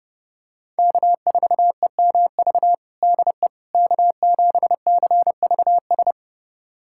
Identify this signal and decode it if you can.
Morse code, heard as a keyed tone, about 30 wpm, 700 Hz. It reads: K4EMV DE K7CVS
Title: USA Callsign DE Callsign - 5 Characters - 30wpm